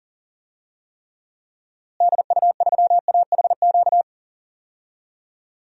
Morse code, heard as a keyed tone, about 40 wpm, 700 Hz. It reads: DU3AHQ
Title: DXCC-100 Call Signs - 6 Characters - Repeated 2x - 40wpm